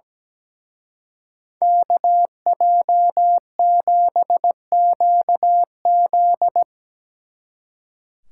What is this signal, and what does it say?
Morse code, heard as a keyed tone, about 19 wpm, 700 Hz. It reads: KJ7QZ
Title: Speed Racing - USA 2x2 Call Signs 17wpm